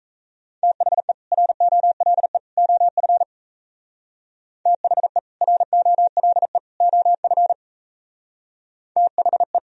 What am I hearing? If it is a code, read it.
Morse code, heard as a keyed tone, about 35 wpm, 700 Hz. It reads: THE ROLE OF THE ROLE OF THE